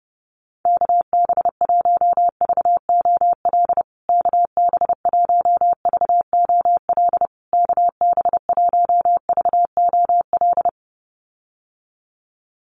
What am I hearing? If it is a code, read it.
Morse code, heard as a keyed tone, about 30 wpm, 700 Hz. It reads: KB1VOL KB1VOL KB1VOL
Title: USA Call Signs Repeated 3x 30wpm